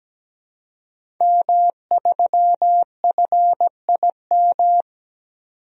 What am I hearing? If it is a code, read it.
Morse code, heard as a keyed tone, about 17 wpm, 700 Hz. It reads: M3FIM
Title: DXCC-100 Call Signs - 5 Characters - 17wpm